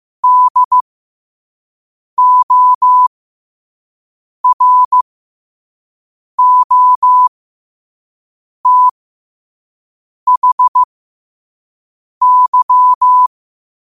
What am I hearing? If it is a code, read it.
Morse code, heard as a keyed tone, about 15 wpm, spaced as for 5 wpm, 1000 Hz. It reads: DOROTHY
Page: qotd-at-5-wpm